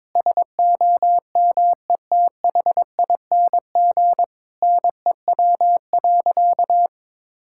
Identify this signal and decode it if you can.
Morse code, heard as a keyed tone, about 22 wpm, 700 Hz. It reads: SOMETHING NEW.
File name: practice / 22wpm / Sentences with 8-Word Phrases - 22wpm